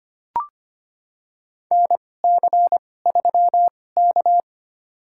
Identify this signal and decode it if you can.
Morse code, heard as a keyed tone, about 25 wpm, spaced as for 18 wpm, 700 Hz. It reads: NC3K
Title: USA Call Signs - 4 Characters - 2x Character Spacing - 25wpm